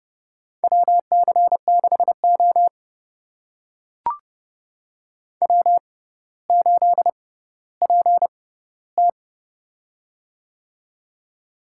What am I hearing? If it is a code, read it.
Morse code, heard as a keyed tone, about 29 wpm, spaced as for 15 wpm, 700 Hz. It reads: WC6O W8PT
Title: USA Call Signs - 4 Characters - 6x Character Spacing - 30wpm